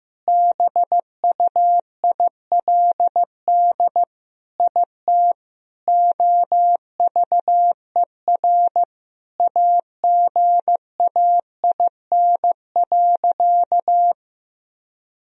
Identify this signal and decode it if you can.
Morse code, heard as a keyed tone, about 16 wpm, 700 Hz. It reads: BUILD IT OVER AGAIN.
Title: Sentences from Top 1000 Words - Review Part 1 - 15wpm